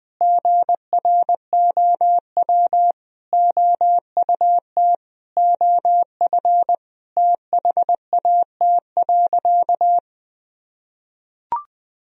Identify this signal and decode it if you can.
Morse code, heard as a keyed tone, about 20 wpm, 700 Hz. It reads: GROW OUT OF THAT.
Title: Sentences from Top 1000 Words - Review Part 1 - 20wpm